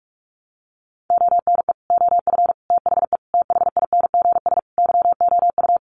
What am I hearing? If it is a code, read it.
Morse code, heard as a keyed tone, about 44 wpm, 700 Hz. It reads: ONE OF THE THINGS YOU
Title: Sets of 5 Words Encore2 45wpm